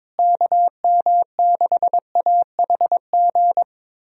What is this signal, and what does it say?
Morse code, heard as a keyed tone, about 22 wpm, 700 Hz. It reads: KM6AHG
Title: USA Call Signs - 6 Characters - Repeated 2x - 22wpm